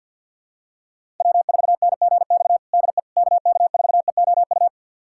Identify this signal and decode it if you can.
Morse code, heard as a keyed tone, about 51 wpm, 700 Hz. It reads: W4NGX DE KK4EOU
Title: USA Callsign DE Callsign 50wpm